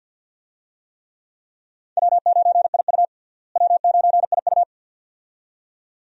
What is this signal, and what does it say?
Morse code, heard as a keyed tone, about 53 wpm, 690 Hz. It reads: W9IU W9IU
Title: USA Call Signs - 4 Characters - Repeated 2x - 50wpm